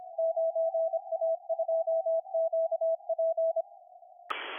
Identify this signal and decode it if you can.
Morse code, heard as a keyed tone, about 26 wpm, 650 Hz. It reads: 9A2QP